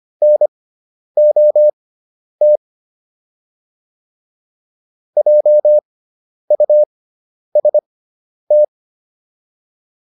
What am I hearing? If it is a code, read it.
Morse code, heard as a keyed tone, about 25 wpm, spaced as for 10 wpm, 600 Hz. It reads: NOT JUST